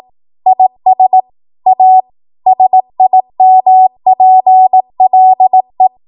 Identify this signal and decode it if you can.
Morse code, heard as a keyed tone, about 18 wpm, 750 Hz. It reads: IS A SIMPLE